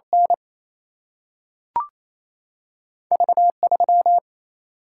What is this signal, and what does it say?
Morse code, heard as a keyed tone, about 28 wpm, 700 Hz. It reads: N V3
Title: Single-Letter Single-Number 28wpm